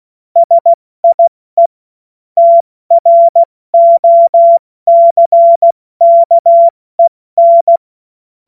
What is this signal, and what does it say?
Morse code, heard as a keyed tone, about 16 wpm, 680 Hz. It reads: SIE TROCKEN
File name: Morsekurs / Witze / Witz 13. April 2021, 70 F80